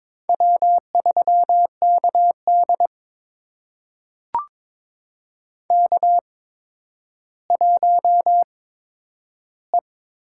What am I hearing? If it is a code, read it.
Morse code, heard as a keyed tone, about 22 wpm, spaced as for 11 wpm, 700 Hz. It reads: W3KD K1E
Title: USA Call Signs - 4 Characters - 8x Character Spacing - 22wpm